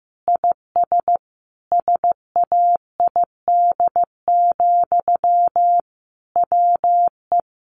Morse code, heard as a keyed tone, about 15 wpm, 700 Hz. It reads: IS SAID, WE
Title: Sentences from Top 1500 Words 15wpm